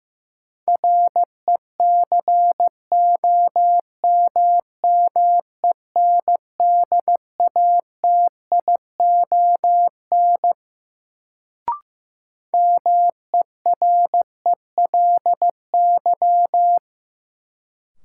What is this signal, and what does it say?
Morse code, heard as a keyed tone, about 15 wpm, 700 Hz. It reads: RECOMMENDATION MERELY
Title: Sentences from Top 1400 Words - New Word Drill Alt - 15wpm